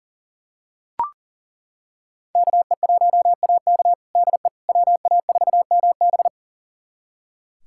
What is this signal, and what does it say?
Morse code, heard as a keyed tone, about 40 wpm, 700 Hz. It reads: KE1AK DE WA4MB
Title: USA Callsign DE Callsign - 5 Characters - 40wpm